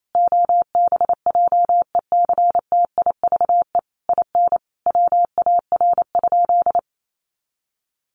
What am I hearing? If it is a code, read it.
Morse code, heard as a keyed tone, about 28 wpm, 700 Hz. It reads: OBJECTIVE IN WAR?
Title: Sentences from Top 1200 Words 28wpm